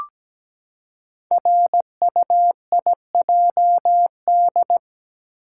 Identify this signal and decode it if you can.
Morse code, heard as a keyed tone, about 17 wpm, 700 Hz. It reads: RUIJD
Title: Code Groups - 5 Letters - 17wpm